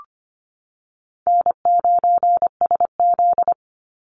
Morse code, heard as a keyed tone, about 25 wpm, 700 Hz. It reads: N9SZ
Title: Easy Call Signs - 25wpm